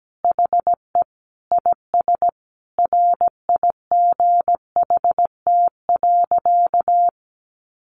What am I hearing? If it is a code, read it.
Morse code, heard as a keyed tone, about 17 wpm, 700 Hz. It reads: HE IS RIGHT.
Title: Sentences from Top 1300 Words 17wpm